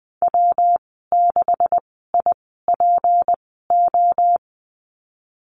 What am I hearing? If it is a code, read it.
Morse code, heard as a keyed tone, about 20 wpm, spaced as for 15 wpm, 700 Hz. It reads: W6IPO
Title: 167 Lesson 27g - Number 7 - Call Signs 20wpm